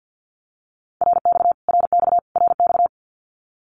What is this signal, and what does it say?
Morse code, heard as a keyed tone, about 50 wpm, 700 Hz. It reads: RX RX RX
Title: Abbreviations Repeated 3x - 50wpm